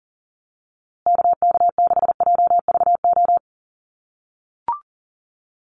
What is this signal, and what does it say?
Morse code, heard as a keyed tone, about 40 wpm, 700 Hz. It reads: KK6JVO
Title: USA Call Signs Repeated 3x 40wpm